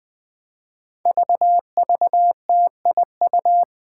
Morse code, heard as a keyed tone, about 21 wpm, 700 Hz. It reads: VVTIU
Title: Five Challenging Letter-Number Combinations 20wpm